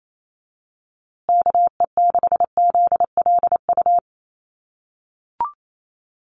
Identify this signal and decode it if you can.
Morse code, heard as a keyed tone, about 28 wpm, 700 Hz. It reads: KE6ZLU
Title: USA Call Signs Repeated 2x 28wpm